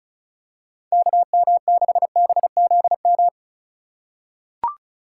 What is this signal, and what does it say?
Morse code, heard as a keyed tone, about 36 wpm, 700 Hz. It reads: KM6BZM
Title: USA Call Signs - 6 Characters - 4x Character Spacing - 35wpm